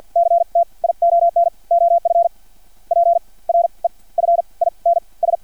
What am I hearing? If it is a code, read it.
Morse code, heard as a keyed tone, about 50 wpm, spaced as for 31 wpm, 680 Hz. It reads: CTION OF PREFINS
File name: archived / CRSnet